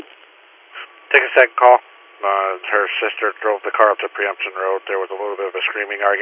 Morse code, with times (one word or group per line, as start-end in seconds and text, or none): none